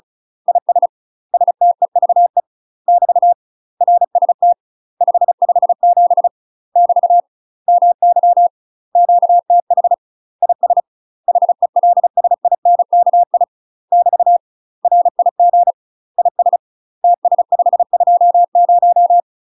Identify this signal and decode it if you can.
Morse code, heard as a keyed tone, about 34 wpm, 700 Hz. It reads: IS STEVE <BT> RST 557 <BT> MY QTH IS HELSINKI <BT> RIG IS TS520
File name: UK MASTER QSOs TRAINING/35WPM